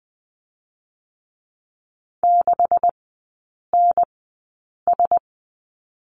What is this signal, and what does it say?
Morse code, heard as a keyed tone, about 20 wpm, spaced as for 8 wpm, 700 Hz. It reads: 6NS
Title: 127 Lesson 21e - Number 6 - Three Characters 20wpm